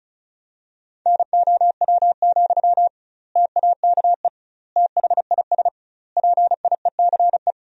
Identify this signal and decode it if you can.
Morse code, heard as a keyed tone, about 34 wpm, 700 Hz. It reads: NOW, TAKE THIS PIECE
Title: 2x Word-Spacing - Sentences from Top 400 Words - 35wpm